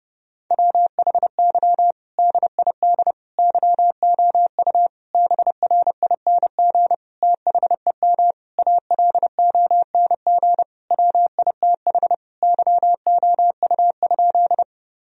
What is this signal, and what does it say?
Morse code, heard as a keyed tone, about 29 wpm, 700 Hz. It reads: WHY DID YOU BRING THEM ALONG WITH YOU?
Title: Sentences from Top 1000 Words - Review Part 3 - 30wpm